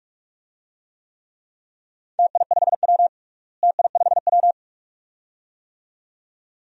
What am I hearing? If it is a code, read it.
Morse code, heard as a keyed tone, about 43 wpm, 700 Hz. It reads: TI5W TI5W
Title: DXCC-100 Call Signs - 4 Characters - Repeated 2x - 45wpm